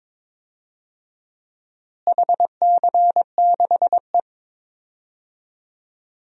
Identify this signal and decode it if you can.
Morse code, heard as a keyed tone, about 22 wpm, 700 Hz. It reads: HC6E
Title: DXCC-100 Call Signs - 4 Characters - Repeated 3x - 22wpm